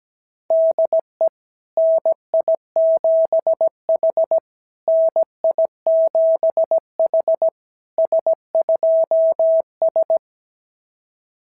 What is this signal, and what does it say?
Morse code, heard as a keyed tone, about 17 wpm, 660 Hz. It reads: DE NI7H NI7H S2S
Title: SOTA QSO Practice 17wpm